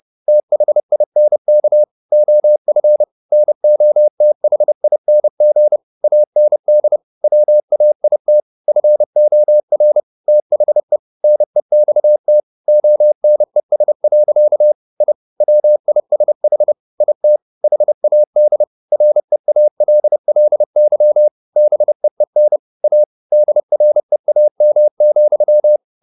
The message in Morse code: THINK OF NOTHING AND WAIT FOR THE NEXT ONES. I WISH IT HAD REALLY BEEN A DREAM,

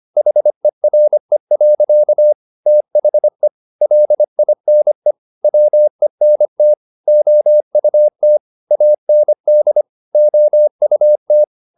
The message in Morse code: HERE. THE LINE WENT OUT AND OUT

